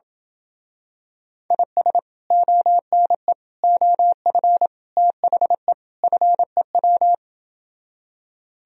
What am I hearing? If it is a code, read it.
Morse code, heard as a keyed tone, about 27 wpm, 700 Hz. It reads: IS ONE OF THE FEW